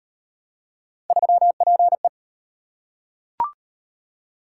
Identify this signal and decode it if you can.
Morse code, heard as a keyed tone, about 38 wpm, 700 Hz. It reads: 3PE